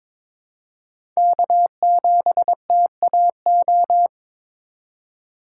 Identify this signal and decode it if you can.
Morse code, heard as a keyed tone, about 22 wpm, 700 Hz. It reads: K7TAO
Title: USA Call Signs - 5 Characters - 4x Character Spacing - 22wpm